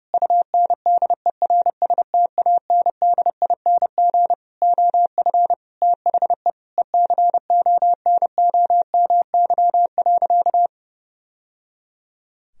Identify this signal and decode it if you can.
Morse code, heard as a keyed tone, about 32 wpm, 700 Hz. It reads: UNDERSTANDING OF THE ECONOMY.
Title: Sentences from Top 1000 Words - Review Part 3 - 30wpm